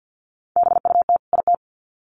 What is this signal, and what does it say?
Morse code, heard as a keyed tone, about 50 wpm, 700 Hz. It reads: BUT IT
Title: Sets of 2 Words 50wpm